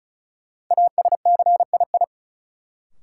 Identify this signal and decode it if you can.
Morse code, heard as a keyed tone, about 36 wpm, 700 Hz. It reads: ASCII
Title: Sets of 5 Words - New Word Drill - 35wpm